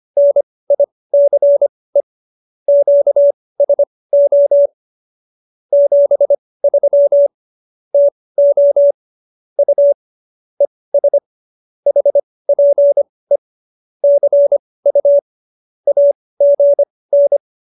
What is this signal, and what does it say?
Morse code, heard as a keyed tone, about 25 wpm, spaced as for 18 wpm, 570 Hz. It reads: NICE QSO 73 TO U ES HPE CU AGN